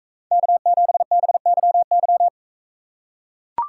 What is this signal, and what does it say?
Morse code, heard as a keyed tone, about 42 wpm, 700 Hz. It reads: K7BYY